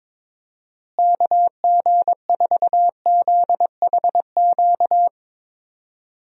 Morse code, heard as a keyed tone, about 21 wpm, 700 Hz. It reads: KG4ZHQ
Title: USA Call Signs - 6 Characters - Repeated 3x - 22wpm